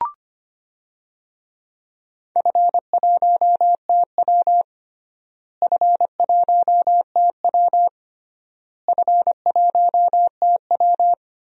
Morse code, heard as a keyed tone, about 25 wpm, 700 Hz. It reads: F1TW F1TW F1TW
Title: DXCC-100 Call Signs - 4 Characters - Repeated 3x - 25wpm